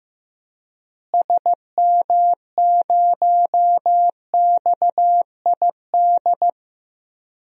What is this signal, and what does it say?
Morse code, heard as a keyed tone, about 15 wpm, 700 Hz. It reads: SM0XID
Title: DXCC-100 Call Signs - 6 Characters - Repeated 2x - 15wpm